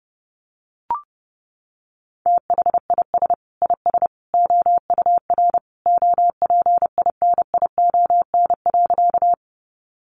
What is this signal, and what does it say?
Morse code, heard as a keyed tone, about 30 wpm, 700 Hz. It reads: THIS IS OUR OPINION.
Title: Sentences from Top 1200 Words 30wpm